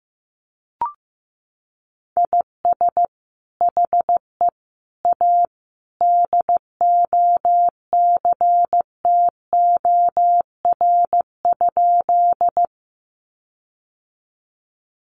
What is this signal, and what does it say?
Morse code, heard as a keyed tone, about 15 wpm, 700 Hz. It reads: IS HE A DOCTOR?
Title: Sentences from Top 1500 Words 15wpm